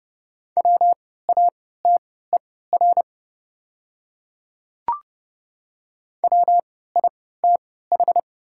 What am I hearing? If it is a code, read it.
Morse code, heard as a keyed tone, about 29 wpm, spaced as for 17 wpm, 700 Hz. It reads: WATER WITH
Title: 3x Character Spacing - Top 200 Words - 30wpm